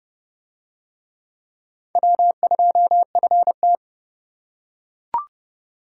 Morse code, heard as a keyed tone, about 28 wpm, 700 Hz. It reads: W2FT